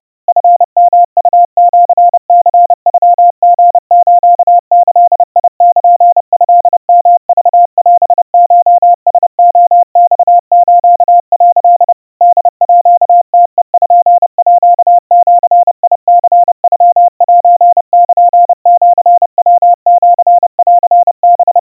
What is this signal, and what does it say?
Morse code (Japanese Wabun code), heard as a keyed tone, about 30 wpm, 700 Hz. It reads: チヨウシニノリスギルトヨクオコラレマス」 ホームページニノセルシヤシンハ
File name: kawa / Morse sample / kamyu 110 cpm 700 Hz Crispy